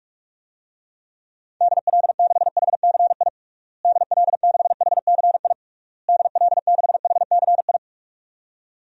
Morse code, heard as a keyed tone, about 46 wpm, 700 Hz. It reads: DL6HCI DL6HCI DL6HCI
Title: DXCC-100 Call Signs - 6 Characters - Repeated 3x - 45wpm